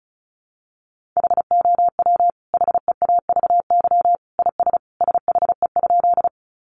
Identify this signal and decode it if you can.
Morse code, heard as a keyed tone, about 34 wpm, 700 Hz. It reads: HOW HEAVY IS SHE?
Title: Sentences from Top 1100 Words 35wpm